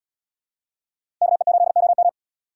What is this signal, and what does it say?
Morse code, heard as a keyed tone, about 74 wpm, 700 Hz. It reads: KE8OM